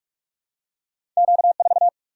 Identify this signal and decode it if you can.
Morse code, heard as a keyed tone, about 46 wpm, 700 Hz. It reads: Q4
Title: Single-Letter Single-Number 45wpm